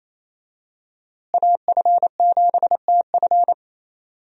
Morse code, heard as a keyed tone, about 28 wpm, 700 Hz. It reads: AF7TF
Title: USA Call Signs - 5 Characters - Repeated 2x - 28wpm